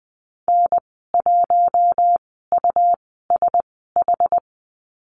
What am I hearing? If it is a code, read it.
Morse code, heard as a keyed tone, about 20 wpm, spaced as for 15 wpm, 700 Hz. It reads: N1USH
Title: 128 Lesson 21g - Number 6 - Call Signs 20wpm